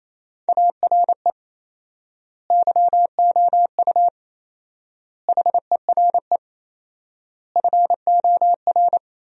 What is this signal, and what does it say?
Morse code, heard as a keyed tone, about 28 wpm, 700 Hz. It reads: ARE YOU HERE FOR
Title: 4x Word-Spacing - Sentences from Top 200 Words - 28wpm